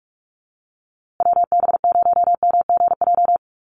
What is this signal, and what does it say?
Morse code, heard as a keyed tone, about 44 wpm, 700 Hz. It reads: WB0MGJ
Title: USA Call Signs Repeated 2x 45wpm